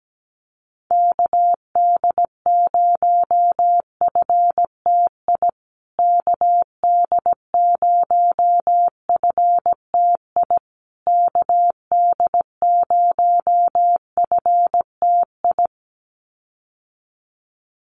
Morse code, heard as a keyed tone, about 17 wpm, 700 Hz. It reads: KD0FTI KD0FTI KD0FTI
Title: USA Call Signs Repeated 3x 17wpm